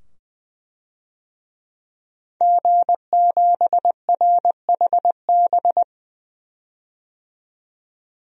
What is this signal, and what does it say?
Morse code, heard as a keyed tone, about 21 wpm, 700 Hz. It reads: G7RHB
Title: DXCC-100 Call Signs - 5 Characters - 20wpm